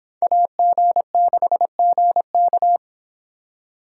Morse code, heard as a keyed tone, about 31 wpm, 700 Hz. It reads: AG6GK